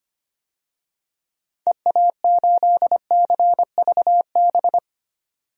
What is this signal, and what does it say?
Morse code, heard as a keyed tone, about 25 wpm, 700 Hz. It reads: EA8CVB
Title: DXCC-100 Call Signs - 6 Characters - 25wpm